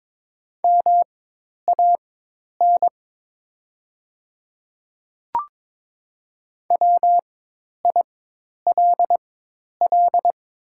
Morse code, heard as a keyed tone, about 22 wpm, spaced as for 10 wpm, 700 Hz. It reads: MAN WILL
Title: 4x Character Spacing - Top 300 Words - 22wpm